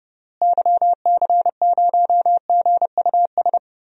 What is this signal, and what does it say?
Morse code, heard as a keyed tone, about 30 wpm, 700 Hz. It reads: YC0GUS
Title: DXCC-100 Call Signs - 6 Characters - Repeated 3x - 30wpm